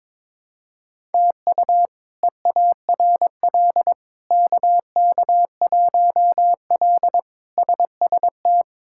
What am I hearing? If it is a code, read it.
Morse code, heard as a keyed tone, about 22 wpm, 700 Hz. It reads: TU EARL KK1L SST
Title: K1USN SST QSO Practice - 22wpm